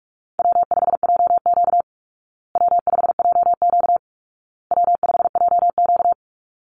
Morse code, heard as a keyed tone, about 44 wpm, 700 Hz. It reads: W5JQ W5JQ W5JQ
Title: Call Signs Repeated 2x - 45wpm